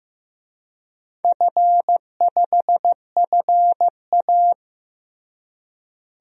Morse code, heard as a keyed tone, about 15 wpm, 700 Hz. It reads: F5FA
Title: DXCC-100 Call Signs - 4 Characters - Repeated 2x - 15wpm